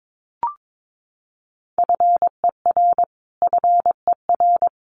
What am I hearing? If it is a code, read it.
Morse code, heard as a keyed tone, about 22 wpm, 700 Hz. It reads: FER FER
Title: Abbreviations Repeated 3x - 22wpm